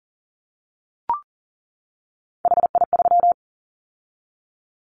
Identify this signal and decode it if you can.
Morse code, heard as a keyed tone, about 40 wpm, 700 Hz. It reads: HI3
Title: Callsign Prefixes for Top 100 DXCC - 40wpm